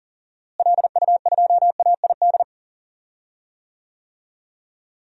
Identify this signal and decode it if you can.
Morse code, heard as a keyed tone, about 38 wpm, 700 Hz. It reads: LU2AID
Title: DXCC-100 Call Signs - 6 Characters - Repeated 3x - 40wpm